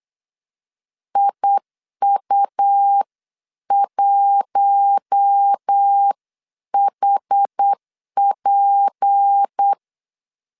Morse code, heard as a keyed tone, about 8 wpm, 800 Hz. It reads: IU1HP
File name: 8wpm